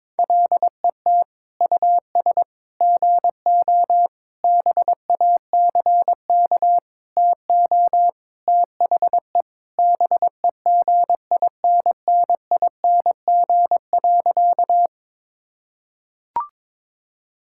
Morse code, heard as a keyed tone, about 22 wpm, 700 Hz. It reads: LET US GO BACK TO THE BEGINNING.
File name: practice / 22wpm / Sentences with 4-Word Phrases - 22wpm